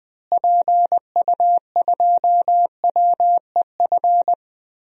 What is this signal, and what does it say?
Morse code, heard as a keyed tone, about 20 wpm, 700 Hz. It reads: PU2WEF
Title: DXCC-100 Call Signs - 6 Characters - Repeated 3x - 20wpm